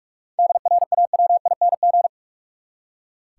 Morse code, heard as a keyed tone, about 46 wpm, 700 Hz. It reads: DRAWING
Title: Seven Letter Words - 45wpm